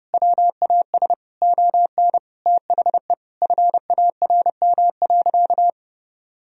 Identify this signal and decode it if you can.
Morse code, heard as a keyed tone, about 30 wpm, 700 Hz. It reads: WAS ON THE FARM.
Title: Sentences from Top 1000 Words - Review Part 2 - 30wpm